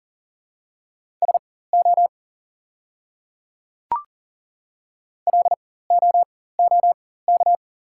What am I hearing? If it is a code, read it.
Morse code, heard as a keyed tone, about 40 wpm, spaced as for 19 wpm, 700 Hz. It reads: SO LOOK
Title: 4x Character Spacing - Top 200 Words - 40wpm